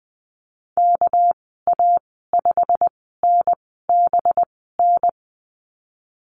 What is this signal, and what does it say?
Morse code, heard as a keyed tone, about 20 wpm, spaced as for 15 wpm, 700 Hz. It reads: KA5NBN
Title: 200 Lesson 32g - Letter k - Call Signs 20wpm